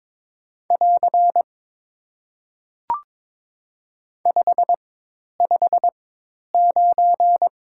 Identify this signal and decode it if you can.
Morse code, heard as a keyed tone, about 23 wpm, spaced as for 10 wpm, 700 Hz. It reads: <AR> 559
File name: Top 90 QSO Elements - 4x Character Spacing - 22wpm